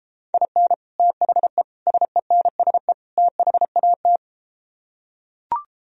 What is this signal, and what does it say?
Morse code, heard as a keyed tone, about 34 wpm, 700 Hz. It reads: IN THE SENSE THAT